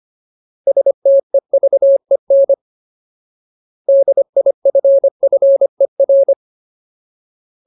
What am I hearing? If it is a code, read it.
Morse code, heard as a keyed tone, about 25 wpm, 550 Hz. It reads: STEVEN DIFFER
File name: Ultimate Word List / 25WPM Ultimate